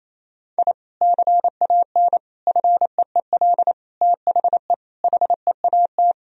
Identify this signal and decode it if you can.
Morse code, heard as a keyed tone, about 28 wpm, 700 Hz. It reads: I CAN FEEL THE HEAT
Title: Sentences from Top 1000 Words - Review Part 3 - 28wpm